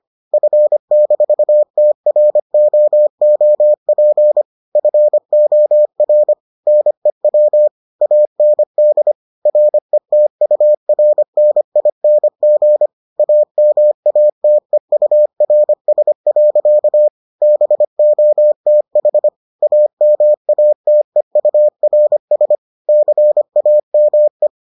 Morse code, heard as a keyed tone, about 25 wpm, 600 Hz. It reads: F-TROOP FOR NEW AND RETURNING AMATEURS. BOTH AMATEURS CAME